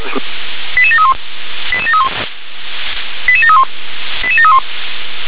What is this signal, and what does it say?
Five-tone selective call (ZVEI), heard as quick runs of five tones, tones of about 70 ms, with speech. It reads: 88731 88731 88731 88731